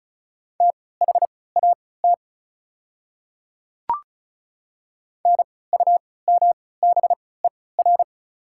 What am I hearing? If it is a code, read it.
Morse code, heard as a keyed tone, about 36 wpm, spaced as for 20 wpm, 700 Hz. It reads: THAT NUMBER
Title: 3x Character Spacing - Top 100 Words - 35wpm